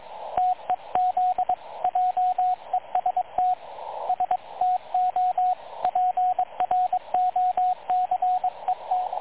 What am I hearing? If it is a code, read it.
Morse code, heard as a keyed tone, about 22 wpm, 730 Hz, with radio noise. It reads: TEZ JEST STO PROCET
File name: dlazaawansowanych / nagraniazpasma / websdr recording 3567.0kHz